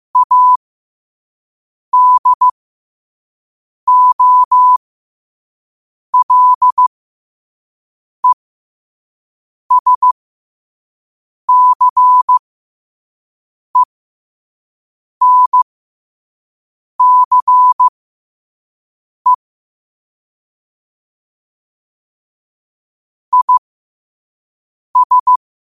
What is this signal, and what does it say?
Morse code, heard as a keyed tone, about 15 wpm, spaced as for 5 wpm, 1000 Hz. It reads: ADOLESCENCE IS
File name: qotdfiles / 5wpm